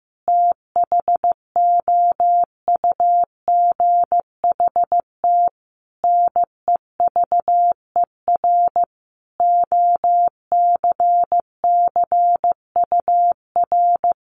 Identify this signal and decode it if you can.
Morse code, heard as a keyed tone, about 15 wpm, 700 Hz. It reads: THOUGHT NEVER OCCUR
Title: Sentences from Top 900 Words 15wpm